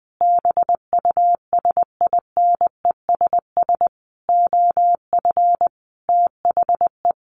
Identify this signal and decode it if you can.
Morse code, heard as a keyed tone, about 20 wpm, 700 Hz. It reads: BUSINESS OF THE
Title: Sentences from Top 1500 Words 20wpm